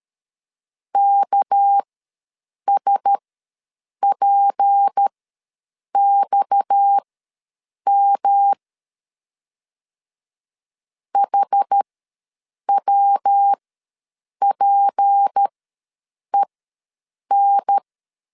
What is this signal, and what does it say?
Morse code, heard as a keyed tone, about 13 wpm, spaced as for 7 wpm, 800 Hz. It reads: KSPXM HWPEN
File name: LEZIONI / 7 GF9Z / 7 G ws 12wpm 3140